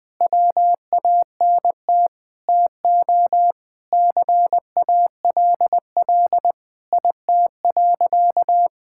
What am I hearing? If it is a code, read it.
Morse code, heard as a keyed tone, about 20 wpm, 700 Hz. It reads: WANT TO CALL IT.
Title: Sentences with 7-Word Phrases - 20wpm